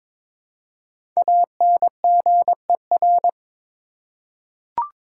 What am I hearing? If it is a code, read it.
Morse code, heard as a keyed tone, about 22 wpm, 700 Hz. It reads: ANGER
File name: 5 Letter Words - First 3 Letters Given - 22wpm